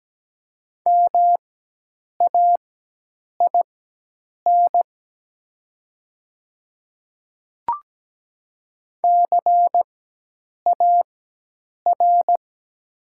Morse code, heard as a keyed tone, about 17 wpm, spaced as for 8 wpm, 700 Hz. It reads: MAIN CAR
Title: 4x Character Spacing - Top 300 Words - 17wpm